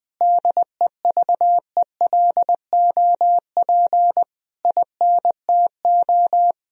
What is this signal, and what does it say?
Morse code, heard as a keyed tone, about 20 wpm, 700 Hz. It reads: DEVELOP INTO
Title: Sentences from Top 1000 Words - Review Part 2 - 20wpm